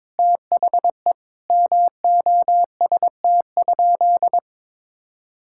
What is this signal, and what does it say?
Morse code, heard as a keyed tone, about 22 wpm, 700 Hz. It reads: THE MOST?
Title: Sentences from Top 1000 Words - Review Part 5 - 22wpm